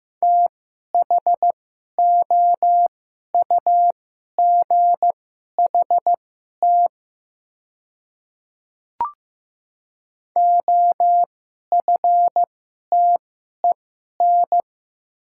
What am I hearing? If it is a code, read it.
Morse code, heard as a keyed tone, about 15 wpm, spaced as for 11 wpm, 700 Hz. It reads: THOUGHT OFTEN
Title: 2x Character Spacing - Top 300 Words - 15wpm